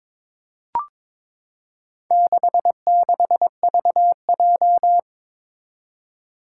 Morse code, heard as a keyed tone, about 22 wpm, 700 Hz. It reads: 66VJ
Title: Four Challenging Letter-Number Combinations 22wpm